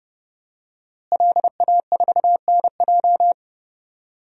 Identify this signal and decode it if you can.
Morse code, heard as a keyed tone, about 30 wpm, 700 Hz. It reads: LA4NJ
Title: DXCC-100 Call Signs - 5 Characters - Repeated 2x - 30wpm